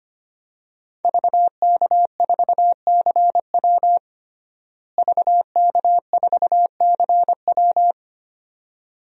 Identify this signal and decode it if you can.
Morse code, heard as a keyed tone, about 25 wpm, 700 Hz. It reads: VK4CW VK4CW
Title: DXCC-100 Call Signs - 5 Characters - Repeated 2x - 25wpm